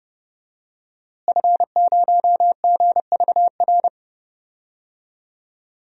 Morse code, heard as a keyed tone, about 29 wpm, 700 Hz. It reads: F0GVR
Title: DXCC-100 Call Signs - 5 Characters - Repeated 3x - 30wpm